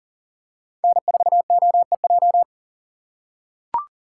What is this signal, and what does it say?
Morse code, heard as a keyed tone, about 38 wpm, 700 Hz. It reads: N4OEJ
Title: USA Call Signs - 5 Characters - 2x Character Spacing - 40wpm